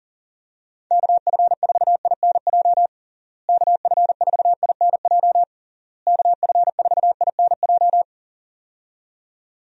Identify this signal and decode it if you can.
Morse code, heard as a keyed tone, about 40 wpm, 700 Hz. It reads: KF4INJ KF4INJ KF4INJ
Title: USA Call Signs - 6 Characters - Repeated 3x - 40wpm